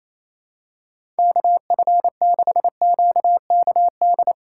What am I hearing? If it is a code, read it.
Morse code, heard as a keyed tone, about 28 wpm, 700 Hz. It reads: KF6QKD
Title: USA Call Signs - 6 Characters - 4x Character Spacing - 28wpm